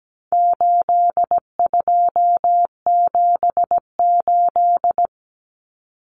Morse code, heard as a keyed tone, about 17 wpm, 700 Hz. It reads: 8278
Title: Numbers - 4-Digit 17wpm